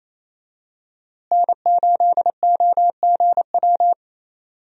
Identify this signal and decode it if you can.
Morse code, heard as a keyed tone, about 28 wpm, 700 Hz. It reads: N8OGW
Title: USA Call Signs - 5 Characters - Repeated 2x - 28wpm